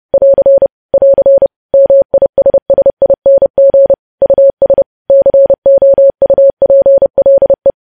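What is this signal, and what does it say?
Morse code, heard as a keyed tone, about 30 wpm, 550 Hz. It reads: <AR> <AR> MISSING US COUPLE
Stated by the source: News headlines sent in Morse at 30 WPM.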